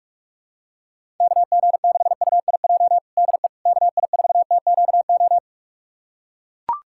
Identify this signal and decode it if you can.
Morse code, heard as a keyed tone, about 42 wpm, 700 Hz. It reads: KG6UIJ DE KI4TQO
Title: USA Callsign DE Callsign - 6 Characters - 45wpm